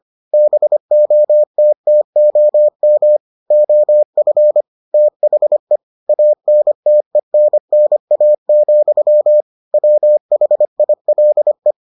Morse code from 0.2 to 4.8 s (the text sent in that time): BOTTOM OF